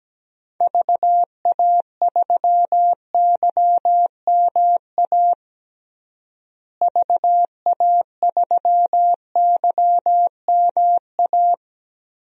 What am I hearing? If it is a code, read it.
Morse code, heard as a keyed tone, about 17 wpm, 700 Hz. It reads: VA3YMA VA3YMA
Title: DXCC-100 Call Signs - 6 Characters - Repeated 2x - 17wpm